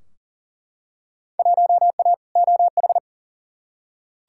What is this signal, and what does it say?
Morse code, heard as a keyed tone, about 40 wpm, 700 Hz. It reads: 1A OH
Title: Field Day Exchanges - 40wpm